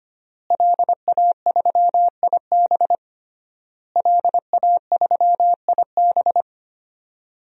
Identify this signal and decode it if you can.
Morse code, heard as a keyed tone, about 25 wpm, 700 Hz. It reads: LA3IB LA3IB
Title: DXCC-100 Call Signs - 5 Characters - Repeated 3x - 25wpm